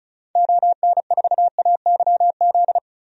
Morse code, heard as a keyed tone, about 36 wpm, 700 Hz. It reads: ON4AYZ
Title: DXCC-100 Call Signs - 6 Characters - 35wpm